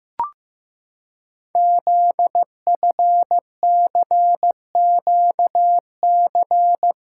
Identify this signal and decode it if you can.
Morse code, heard as a keyed tone, about 15 wpm, 700 Hz. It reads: ZFCQC
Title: Code Groups - 5 Letters - 15wpm